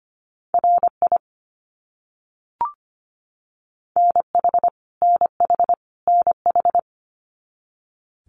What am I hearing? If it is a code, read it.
Morse code, heard as a keyed tone, about 25 wpm, 700 Hz. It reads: RI NH NH NH
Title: US States Repeated 3x 25wpm